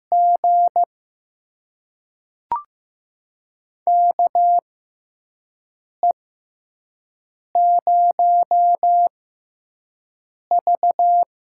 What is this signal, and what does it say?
Morse code, heard as a keyed tone, about 15 wpm, spaced as for 5 wpm, 700 Hz. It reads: G KE0V
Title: USA Call Signs - 6 Characters - 6x Character Spacing - 15wpm